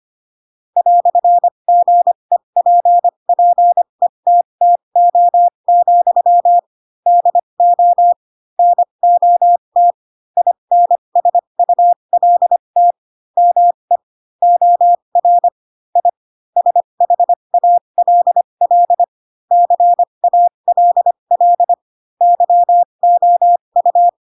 Morse code, heard as a keyed tone, about 25 wpm, spaced as for 22 wpm, 700 Hz. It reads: "GEPPETTO, DO NOT INSULT ME OR I SHALL CALL YOU
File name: Pinocchio / Pinocchio ch2 18 25 ik1hge